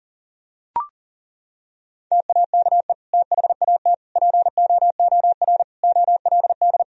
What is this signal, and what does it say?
Morse code, heard as a keyed tone, about 37 wpm, 700 Hz. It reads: TAKE THAT POOR OLD
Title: Sentences from Top 1000 Words - Review Part 2 - 40wpm